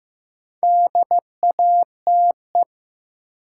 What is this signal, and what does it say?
Morse code, heard as a keyed tone, about 15 wpm, 700 Hz. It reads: DATE
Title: Sentences from Top 1100 Words - New Word Drill Alt - 15wpm